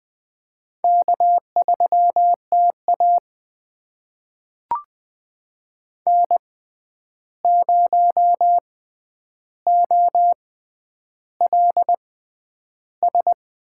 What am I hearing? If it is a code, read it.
Morse code, heard as a keyed tone, about 20 wpm, spaced as for 10 wpm, 700 Hz. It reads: K3TA N0OLS
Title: USA Call Signs - 6x Character Spacing - 20wpm